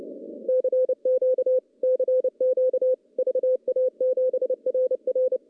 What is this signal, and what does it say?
Morse code, heard as a keyed tone, about 30 wpm, 510 Hz. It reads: CQ CQ VA7RR
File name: contests / CW-SS-k5tr- -0900